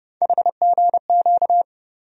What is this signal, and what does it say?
Morse code, heard as a keyed tone, about 30 wpm, 700 Hz. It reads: HGQ
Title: USA Call Sign Postfixes - 3 Letters 30wpm